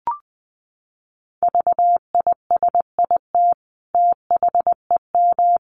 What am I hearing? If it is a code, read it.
Morse code, heard as a keyed tone, about 20 wpm, 700 Hz. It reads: VISIT THEM
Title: Sentences from Top 900 Words 20wpm